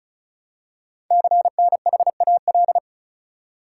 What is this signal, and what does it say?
Morse code, heard as a keyed tone, about 36 wpm, 690 Hz. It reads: CNHAL
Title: Code Groups - 5 Letters - 35wpm